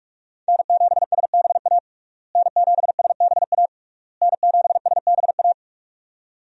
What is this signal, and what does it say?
Morse code, heard as a keyed tone, about 46 wpm, 700 Hz. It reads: N7SBA N7SBA N7SBA
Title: USA Call Signs - 5 Characters - Repeated 3x - 45wpm